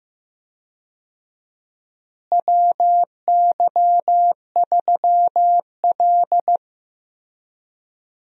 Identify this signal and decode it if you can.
Morse code, heard as a keyed tone, about 16 wpm, 700 Hz. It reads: WY3L
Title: USA Call Signs - 4 Characters - Repeated 3x - 15wpm